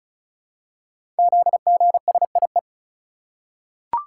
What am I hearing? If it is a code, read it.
Morse code, heard as a keyed tone, about 36 wpm, 700 Hz. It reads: ZGSIE
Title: Code Groups - 5 Letters - 35wpm